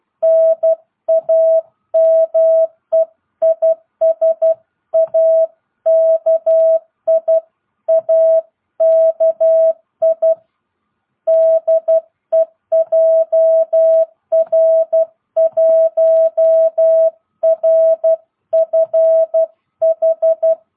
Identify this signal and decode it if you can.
Morse code, heard as a keyed tone, about 12 wpm, 660 Hz. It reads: NAME IS AKI AKI DE JR1RFH
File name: jh8cbh / 2022cw